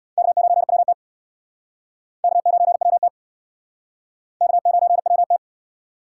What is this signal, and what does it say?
Morse code, heard as a keyed tone, about 68 wpm, 700 Hz. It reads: K9WT K9WT K9WT